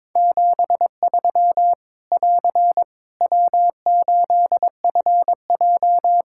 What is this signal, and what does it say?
Morse code, heard as a keyed tone, about 22 wpm, 700 Hz. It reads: 73 <AR> W8FJ
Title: Practice QSO - Exchange 4 - Ending - 22wpm